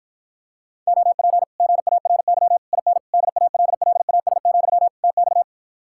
Morse code, heard as a keyed tone, about 54 wpm, 700 Hz. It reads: OP GARY IN DALLAS, TX